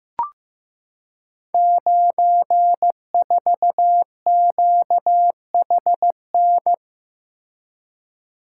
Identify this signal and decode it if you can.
Morse code, heard as a keyed tone, about 15 wpm, 700 Hz. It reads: 94QHN
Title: Code Groups - 5 Letter:Numbers - 15wpm